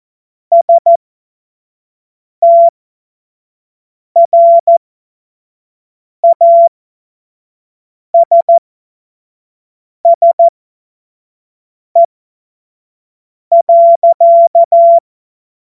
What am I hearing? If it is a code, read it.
Morse code, heard as a keyed tone, about 14 wpm, spaced as for 5 wpm, 680 Hz. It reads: STRASSE.
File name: Morsekurs / Witze / Witz 03. Mai 2022 25 F70